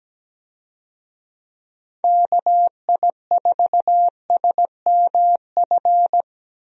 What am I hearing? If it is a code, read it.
Morse code, heard as a keyed tone, about 17 wpm, 700 Hz. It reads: KI4SMF
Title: USA Call Signs - 6 Characters - Repeated 3x - 17wpm